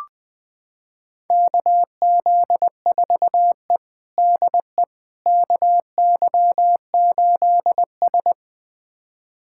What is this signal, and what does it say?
Morse code, heard as a keyed tone, about 20 wpm, 700 Hz. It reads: KZ4E DE KY8S
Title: USA Callsign DE Callsign - 4 Characters - 20wpm